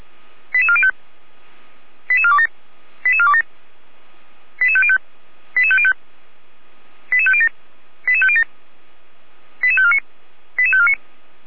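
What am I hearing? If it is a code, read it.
Five-tone selective call (ZVEI), heard as quick runs of five tones, tones of about 70 ms. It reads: 88405 88427 88427 88505 88505 88507 88507 88549 88549
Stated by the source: ZVEI five-tone call received by SDR.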